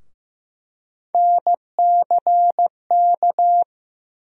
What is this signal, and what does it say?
Morse code, heard as a keyed tone, about 15 wpm, 700 Hz. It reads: NCK
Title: USA Call Sign Postfixes - 3 Letters 15wpm